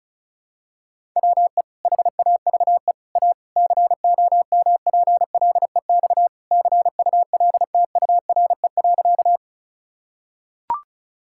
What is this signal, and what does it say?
Morse code, heard as a keyed tone, about 36 wpm, 700 Hz. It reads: WE HAVE A COMPLEX CULTURE.
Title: Sentences from Top 1000 Words - Review Part 6 - 35wpm